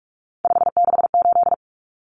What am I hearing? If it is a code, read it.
Morse code, heard as a keyed tone, about 44 wpm, 700 Hz. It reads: 568
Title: Numbers - 3-Digit 45wpm